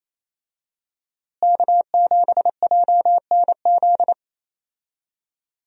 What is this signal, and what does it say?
Morse code, heard as a keyed tone, about 28 wpm, 700 Hz. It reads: K7JNZ
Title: USA Call Signs - 5 Characters - Repeated 3x - 28wpm